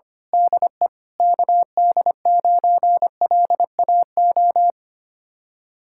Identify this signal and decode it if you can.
Morse code, heard as a keyed tone, about 24 wpm, 700 Hz. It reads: DE KD9LAO
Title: Speed-Racing Common QSO Phrases Encore - 25wpm